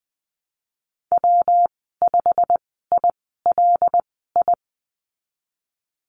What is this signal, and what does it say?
Morse code, heard as a keyed tone, about 20 wpm, spaced as for 15 wpm, 700 Hz. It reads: W5ILI
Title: 154 Lesson 25g - Letter p - Call Signs 20wpm